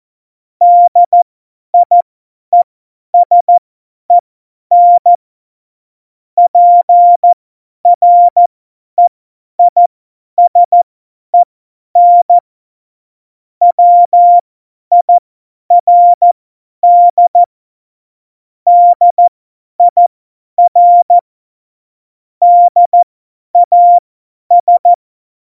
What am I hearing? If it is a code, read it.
Morse code, heard as a keyed tone, about 14 wpm, spaced as for 10 wpm, 700 Hz. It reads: DIESEN PREISEN WIRD DIR DAS
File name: Morsekurs / Witze / Witz 29. Dezember 2020, 50 F70